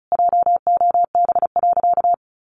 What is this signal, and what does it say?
Morse code, heard as a keyed tone, about 34 wpm, 700 Hz. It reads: JOB.
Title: Sentences from Top 1100 Words 35wpm